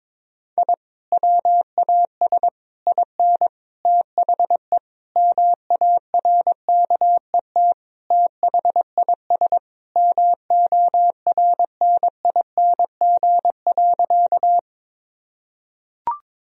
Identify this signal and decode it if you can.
Morse code, heard as a keyed tone, about 22 wpm, 700 Hz. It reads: I WAS IN THE MARKET THIS MORNING.
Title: Sentences from Top 1000 Words - Review Part 3 - 22wpm